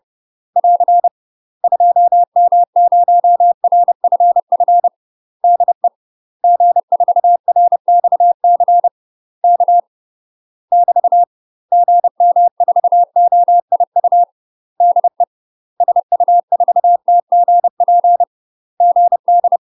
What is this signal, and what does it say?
Morse code, heard as a keyed tone, about 30 wpm, 700 Hz. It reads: <AR> 2M0RFF DE G4RXC K <BT> GM4OIU DE SU4TGP GD
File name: UK MASTER QSOs TRAINING / 30WPM / UK MASTER QSOs 0050